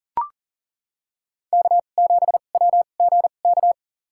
Morse code, heard as a keyed tone, about 40 wpm, spaced as for 29 wpm, 700 Hz. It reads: K7WGK
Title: USA Call Signs - 5 Characters - 2x Character Spacing - 40wpm